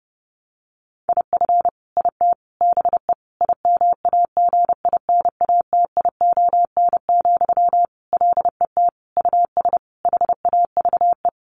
Morse code, heard as a keyed tone, about 30 wpm, 700 Hz. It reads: IF IT BE IMAGINATION, LET US HAVE
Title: Sentences from Top 1300 Words 30wpm